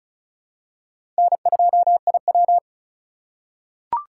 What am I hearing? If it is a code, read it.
Morse code, heard as a keyed tone, about 34 wpm, 700 Hz. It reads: N2IW
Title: USA Call Signs - 4 Characters - 3x Character Spacing - 35wpm